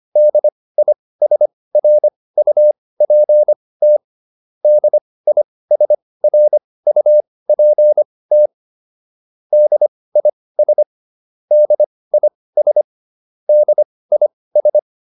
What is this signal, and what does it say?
Morse code, heard as a keyed tone, about 25 wpm, spaced as for 18 wpm, 600 Hz. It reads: DISRUPT DISRUPT DIS DIS DIS